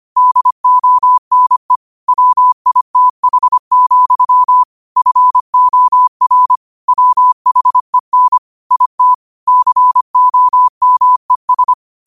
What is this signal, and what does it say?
Morse code, heard as a keyed tone, about 25 wpm, 1000 Hz. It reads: DONE WITH, FOR WHEN IT COMES
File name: qotdfiles / 25wpm